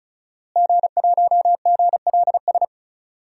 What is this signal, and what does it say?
Morse code, heard as a keyed tone, about 36 wpm, 700 Hz. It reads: G1GLS
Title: DXCC-100 Call Signs - 5 Characters - Repeated 2x - 35wpm